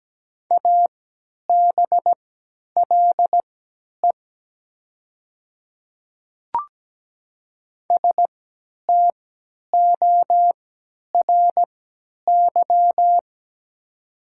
Morse code, heard as a keyed tone, about 17 wpm, spaced as for 10 wpm, 700 Hz. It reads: ABLE STORY
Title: 3x Character Spacing - Top 500 Words - 17wpm